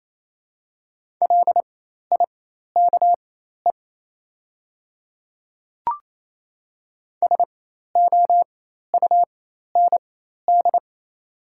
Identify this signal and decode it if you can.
Morse code, heard as a keyed tone, about 28 wpm, spaced as for 13 wpm, 700 Hz. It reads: LIKE SOUND
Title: 4x Character Spacing - Top 200 Words - 28wpm